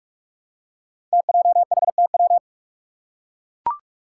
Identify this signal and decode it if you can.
Morse code, heard as a keyed tone, about 44 wpm, 700 Hz. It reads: TJHTW
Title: Five Challenging Letter-Number Combinations 45wpm